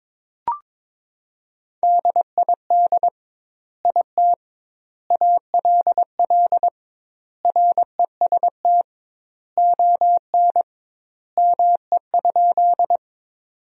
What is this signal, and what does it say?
Morse code, heard as a keyed tone, about 22 wpm, 700 Hz. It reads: DID IT ALL REST ON ME?
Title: 2x Word-Spacing - Sentences from Top 500 Words - 22wpm